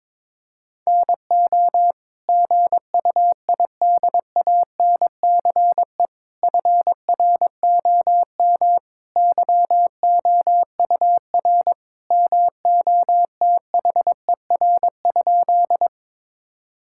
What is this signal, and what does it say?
Morse code, heard as a keyed tone, about 22 wpm, 700 Hz. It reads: NO GUIDANCE FROM YOUR MOTHER?
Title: Sentences from Top 1600 Words 22wpm